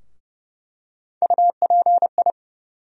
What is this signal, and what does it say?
Morse code, heard as a keyed tone, about 30 wpm, 700 Hz. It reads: UPI
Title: USA Call Sign Postfixes - 3 Letters 30wpm